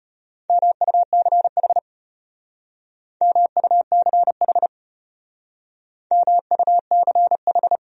Morse code, heard as a keyed tone, about 33 wpm, 700 Hz. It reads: MUCH MUCH MUCH